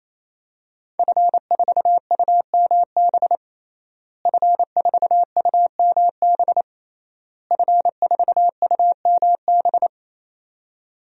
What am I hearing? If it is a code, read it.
Morse code, heard as a keyed tone, about 28 wpm, 700 Hz. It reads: F4UMB F4UMB F4UMB
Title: DXCC-100 Call Signs - 5 Characters - Repeated 3x - 28wpm